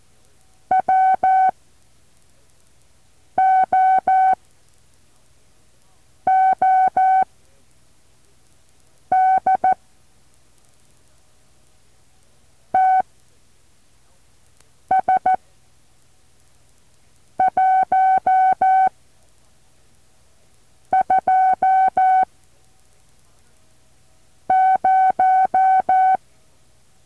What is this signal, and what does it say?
Morse code, heard as a keyed tone, about 14 wpm, spaced as for 4 wpm, 760 Hz. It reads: WOOD TS120
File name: intl morse code2